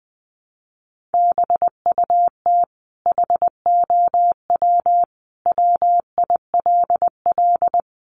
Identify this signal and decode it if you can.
Morse code, heard as a keyed tone, about 20 wpm, 700 Hz. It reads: BUT HOW WILL
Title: Sentences from Top 1100 Words 20wpm